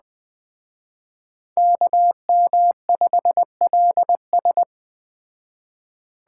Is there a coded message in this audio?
Yes: Morse code from 1.6 to 4.6 s, heard as a keyed tone, about 20 wpm, 700 Hz. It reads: KM5LS